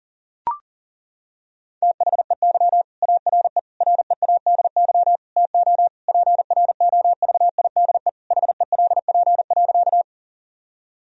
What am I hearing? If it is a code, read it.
Morse code, heard as a keyed tone, about 40 wpm, 700 Hz. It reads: THEY ARE READY TO PROVIDE HELP.
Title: Sentences from Top 1000 Words - Review Part 3 - 40wpm